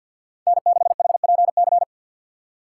Morse code, heard as a keyed tone, about 47 wpm, 690 Hz. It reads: N6HPC
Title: USA Call Signs - 5 Characters - 3x Character Spacing - 50wpm